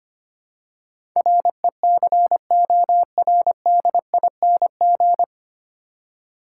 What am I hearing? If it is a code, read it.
Morse code, heard as a keyed tone, about 25 wpm, 700 Hz. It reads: RECORDING
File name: practice / 25wpm / 9 Letter Words - First 3 Letters Given - 25wpm